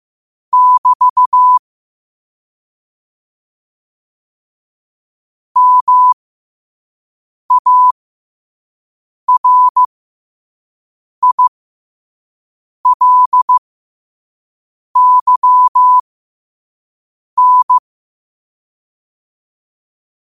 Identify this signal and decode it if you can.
Morse code, heard as a keyed tone, about 15 wpm, spaced as for 5 wpm, 1000 Hz. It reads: <BT> MARILYN
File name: qotdfiles / 5wpm